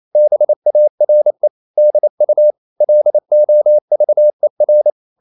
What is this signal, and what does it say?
Morse code, heard as a keyed tone, about 28 wpm, 600 Hz. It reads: BARE DU LOVER
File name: Ferie QRQ 28wpm